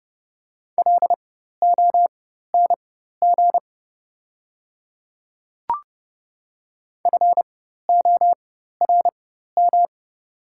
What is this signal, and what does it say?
Morse code, heard as a keyed tone, about 30 wpm, spaced as for 14 wpm, 700 Hz. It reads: LONG FORM
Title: 4x Character Spacing - Top 200 Words - 30wpm